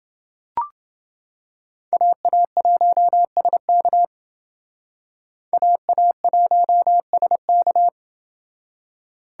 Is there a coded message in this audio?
Yes: Morse code from 1.9 to 7.9 s, heard as a keyed tone, about 28 wpm, 700 Hz. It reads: AA1SK AA1SK